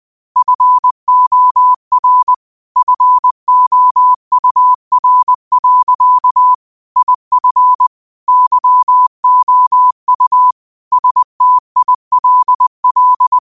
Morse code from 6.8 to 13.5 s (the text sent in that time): IF YOU STILL